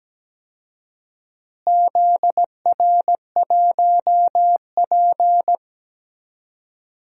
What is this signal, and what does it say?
Morse code, heard as a keyed tone, about 17 wpm, 700 Hz. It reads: ZR1P
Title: DXCC-100 Call Signs - 4 Characters - Repeated 3x - 17wpm